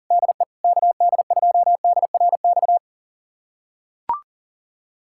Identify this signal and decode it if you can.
Morse code, heard as a keyed tone, about 37 wpm, 700 Hz. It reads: DE KD2DRX
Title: USA Callsign DE Callsign - 6 Characters - 40wpm